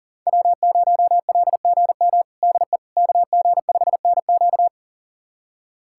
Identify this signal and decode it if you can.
Morse code, heard as a keyed tone, about 37 wpm, 700 Hz. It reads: W0LGM DE KG5NQ
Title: USA Callsign DE Callsign - 5 Characters - 40wpm